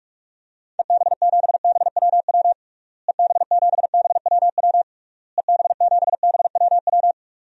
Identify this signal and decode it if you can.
Morse code, heard as a keyed tone, about 46 wpm, 700 Hz. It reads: EB7BWW EB7BWW EB7BWW
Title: DXCC-100 Call Signs - 6 Characters - Repeated 3x - 45wpm